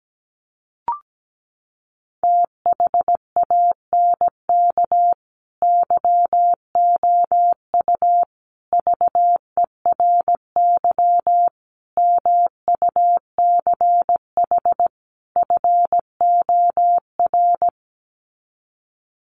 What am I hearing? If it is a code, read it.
Morse code, heard as a keyed tone, about 17 wpm, 700 Hz. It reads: THANK YOU VERY MUCH FOR
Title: Sets of 5 Words 17wpm